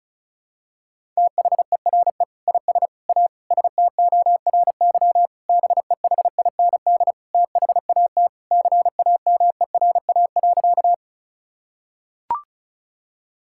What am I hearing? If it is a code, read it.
Morse code, heard as a keyed tone, about 36 wpm, 700 Hz. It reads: THERE IS A STORY BEHIND THAT CAMERA.
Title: Sentences from Top 1000 Words - Review Part 5 - 35wpm